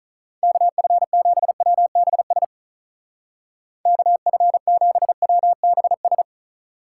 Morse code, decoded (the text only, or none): KF7WBS KF7WBS